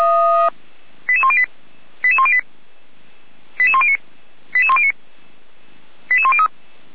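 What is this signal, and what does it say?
Five-tone selective call (ZVEI), heard as quick runs of five tones, tones of about 70 ms. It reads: 88108 88108 88109 88109 88103
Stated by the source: ZVEI five-tone call 88383 received by SDR.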